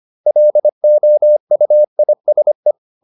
Morse code, heard as a keyed tone, about 25 wpm, 600 Hz. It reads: LOUISE
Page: Ferie 25 wpm